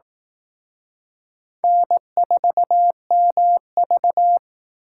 Morse code, heard as a keyed tone, about 18 wpm, 700 Hz. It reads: N4MV
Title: Speed Racing - USA Call Signs 15wpm